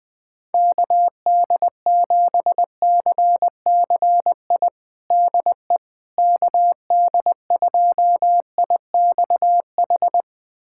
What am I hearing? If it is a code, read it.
Morse code, heard as a keyed tone, about 20 wpm, 700 Hz. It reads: KD7CCI DE KD2IXH
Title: USA Callsign DE Callsign - 6 Characters - 20wpm